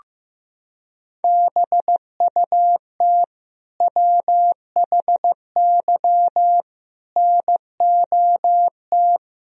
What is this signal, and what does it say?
Morse code, heard as a keyed tone, about 15 wpm, 700 Hz. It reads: BUT WHY NOT
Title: Sentences from Top 1000 Words - Review Part 1 - 15wpm